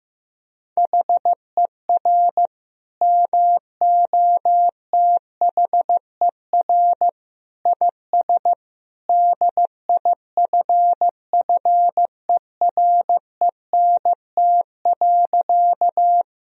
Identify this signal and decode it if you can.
Morse code, heard as a keyed tone, about 16 wpm, 700 Hz. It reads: HER MOTHER IS DIFFERENT.
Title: Sentences from Top 1000 Words - Review Part 3 - 15wpm